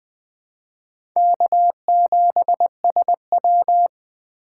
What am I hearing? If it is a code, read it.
Morse code, heard as a keyed tone, about 20 wpm, 700 Hz. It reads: K7SW
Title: USA Call Signs - 4 Characters - 4x Character Spacing - 20wpm